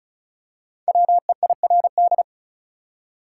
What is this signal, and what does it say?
Morse code, heard as a keyed tone, about 36 wpm, 700 Hz. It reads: WEIRD
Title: Five Letter Words - 35wpm